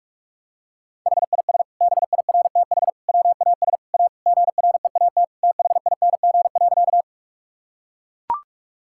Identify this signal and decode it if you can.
Morse code, heard as a keyed tone, about 44 wpm, 690 Hz. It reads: HIS BIRTH WAS A GREAT THING.
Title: Sentences from Top 1000 Words - Review Part 5 - 45wpm